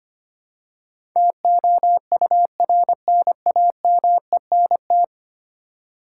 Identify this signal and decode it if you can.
Morse code, heard as a keyed tone, about 25 wpm, 690 Hz. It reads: TOURNAMENT
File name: 10 Letter Words - First 3 Letters Given - 25wpm